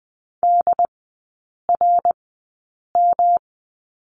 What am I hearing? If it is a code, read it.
Morse code, heard as a keyed tone, about 20 wpm, spaced as for 8 wpm, 700 Hz. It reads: DRM
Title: 127 Lesson 21e - Number 6 - Three Characters 20wpm